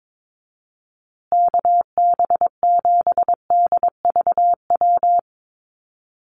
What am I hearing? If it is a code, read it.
Morse code, heard as a keyed tone, about 22 wpm, 700 Hz. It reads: KB7DVW
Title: USA Call Signs Repeated 3x 22wpm